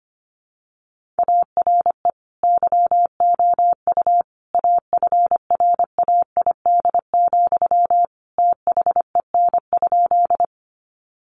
Morse code, heard as a keyed tone, about 25 wpm, 700 Hz. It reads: ARE YOU AFRAID, THEN?
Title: Sentences from Top 1500 Words 25wpm